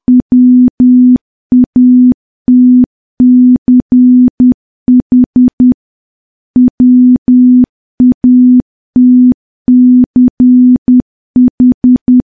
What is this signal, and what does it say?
Morse code, heard as a keyed tone, about 10 wpm, 250 Hz. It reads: WATCH WATCH
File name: morse / morse-training-10WPM